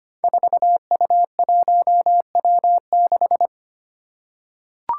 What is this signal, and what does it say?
Morse code, heard as a keyed tone, about 25 wpm, 700 Hz. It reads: 4U1W6
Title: Five Challenging Letter-Number Combinations 25wpm